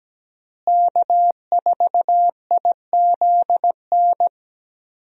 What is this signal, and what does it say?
Morse code, heard as a keyed tone, about 17 wpm, 700 Hz. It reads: K4IZN
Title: USA Call Signs - 5 Characters - Repeated 2x - 17wpm